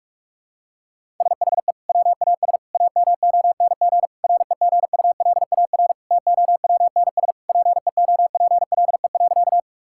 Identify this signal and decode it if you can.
Morse code, heard as a keyed tone, about 46 wpm, 700 Hz. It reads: SHE WAS AMONG REGULAR TOWNSPEOPLE.
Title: Sentences from Top 1000 Words - Review Part 6 - 45wpm